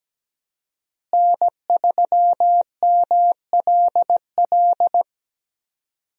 Morse code, heard as a keyed tone, about 17 wpm, 700 Hz. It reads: N3MLL
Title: USA Call Signs - 5 Characters - Repeated 3x - 17wpm